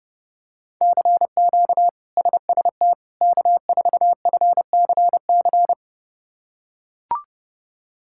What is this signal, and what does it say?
Morse code, heard as a keyed tone, about 32 wpm, 700 Hz. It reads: CQ SST K4FCC
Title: K1USN SST QSO Practice - 30wpm